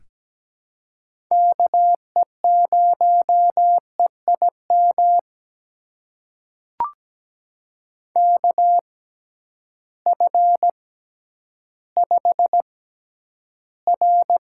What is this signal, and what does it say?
Morse code, heard as a keyed tone, about 17 wpm, spaced as for 10 wpm, 700 Hz. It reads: KE0EIM KF5R
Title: USA Call Signs - 6 Characters - 6x Character Spacing - 17wpm